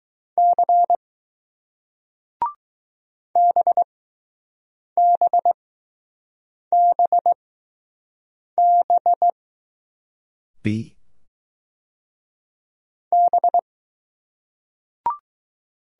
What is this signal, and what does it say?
Morse code, heard as a keyed tone, about 20 wpm, spaced as for 6 wpm, 700 Hz. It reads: C BBBB B